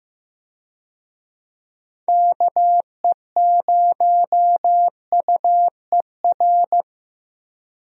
Morse code, heard as a keyed tone, about 15 wpm, 700 Hz. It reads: KE0UER
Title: USA Call Signs - 6 Characters - Repeated 3x - 15wpm